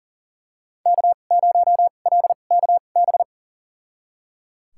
Morse code, heard as a keyed tone, about 38 wpm, spaced as for 28 wpm, 700 Hz. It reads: K0LKB
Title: USA Call Signs - 2x Character Spacing - 40wpm